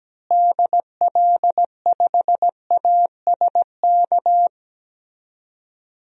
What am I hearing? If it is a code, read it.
Morse code, heard as a keyed tone, about 17 wpm, 700 Hz. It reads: DL5ASK
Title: DXCC-100 Call Signs - 6 Characters - Repeated 3x - 17wpm